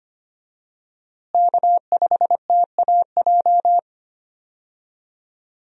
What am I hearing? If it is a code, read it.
Morse code, heard as a keyed tone, about 25 wpm, 700 Hz. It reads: K5TAJ
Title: USA Call Signs - 5 Characters - Repeated 3x - 25wpm